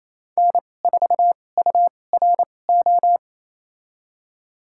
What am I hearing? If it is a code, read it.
Morse code, heard as a keyed tone, about 28 wpm, spaced as for 20 wpm, 700 Hz. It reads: N4URO
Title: USA Call Signs - 5 Characters - 2x Character Spacing - 28wpm